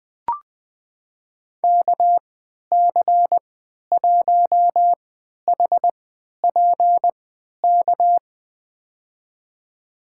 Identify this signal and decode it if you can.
Morse code, heard as a keyed tone, about 20 wpm, spaced as for 11 wpm, 700 Hz. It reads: KC1HPK
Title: USA Call Signs - 6 Characters - 3x Character Spacing - 20wpm